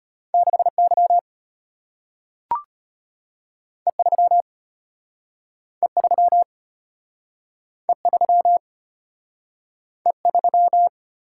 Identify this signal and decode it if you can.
Morse code, heard as a keyed tone, about 34 wpm, 700 Hz. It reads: 6Y E3 E3 E3 E3